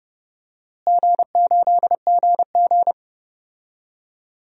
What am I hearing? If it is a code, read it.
Morse code, heard as a keyed tone, about 30 wpm, 700 Hz. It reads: G8GG
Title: DXCC-100 Call Signs - 4 Characters - Repeated 2x - 30wpm